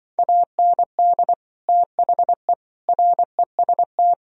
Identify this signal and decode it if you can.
Morse code, heard as a keyed tone, about 24 wpm, 700 Hz. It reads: AND THE REST
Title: Speed-Racing - Sets of 5 Words 20wpm